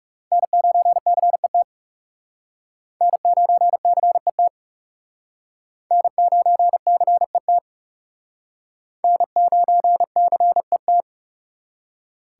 Morse code, heard as a keyed tone, about 39 wpm, 700 Hz. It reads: N9CET N9CET N9CET N9CET